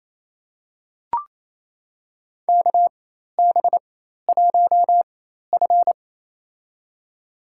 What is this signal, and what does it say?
Morse code, heard as a keyed tone, about 28 wpm, spaced as for 13 wpm, 700 Hz. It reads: KB1F
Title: USA Call Signs - 4 Characters - 4x Character Spacing - 28wpm